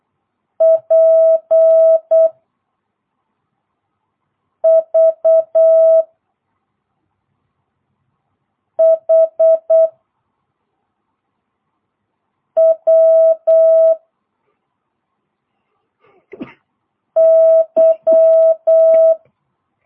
Morse code, heard as a keyed tone, about 8 wpm, 650 Hz. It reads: P V H W Y